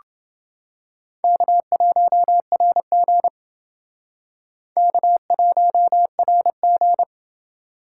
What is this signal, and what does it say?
Morse code, heard as a keyed tone, about 29 wpm, 700 Hz. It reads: K1RG K1RG